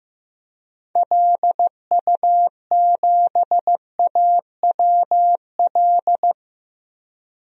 Morse code, heard as a keyed tone, about 16 wpm, 700 Hz. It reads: LU7AWL
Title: DXCC-100 Call Signs - 6 Characters - 15wpm